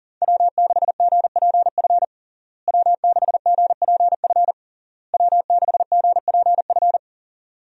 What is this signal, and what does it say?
Morse code, heard as a keyed tone, about 40 wpm, 700 Hz. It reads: W6GPF W6GPF W6GPF
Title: USA Call Signs - 5 Characters - Repeated 3x - 40wpm